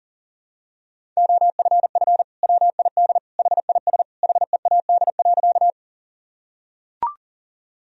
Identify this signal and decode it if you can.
Morse code, heard as a keyed tone, about 40 wpm, 700 Hz. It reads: OFF WID HIS HEAD.
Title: Sentences from Top 300 Words and Abbreviations 40wpm